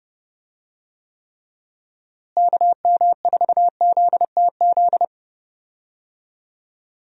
Morse code, heard as a keyed tone, about 30 wpm, 700 Hz. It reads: KM4ZTZ
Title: USA Call Signs - 6 Characters - Repeated 3x - 30wpm